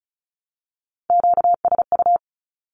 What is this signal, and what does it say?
Morse code, heard as a keyed tone, about 36 wpm, 700 Hz. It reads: QSU
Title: Q-Codes with Spoken Meaning - 35wpm